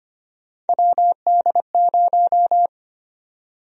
Morse code, heard as a keyed tone, about 25 wpm, 700 Hz. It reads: WD0
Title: USA Call Sign Prefixes - Repeated 2x - 25wpm